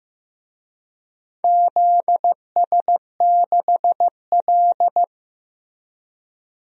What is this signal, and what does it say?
Morse code, heard as a keyed tone, about 15 wpm, 700 Hz. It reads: ZS6L
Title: DXCC-100 Call Signs - 4 Characters - Repeated 2x - 15wpm